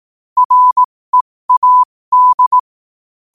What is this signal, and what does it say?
Morse code, heard as a keyed tone, about 18 wpm, spaced as for 15 wpm, 1000 Hz. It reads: READ
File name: qotdfiles / 15wpm